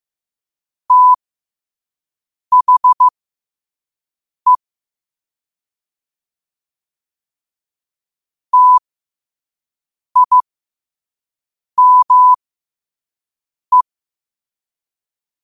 Morse code, heard as a keyed tone, about 15 wpm, spaced as for 5 wpm, 1000 Hz. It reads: THE TIME
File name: qotdfiles / 5wpm